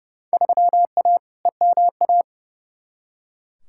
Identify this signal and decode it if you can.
Morse code, heard as a keyed tone, about 30 wpm, 700 Hz. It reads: 3A EMA
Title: Field Day Exchanges - 30wpm